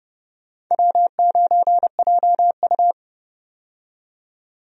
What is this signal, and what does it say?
Morse code, heard as a keyed tone, about 30 wpm, 700 Hz. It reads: W9JU
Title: USA Call Signs - 4 Characters - 30wpm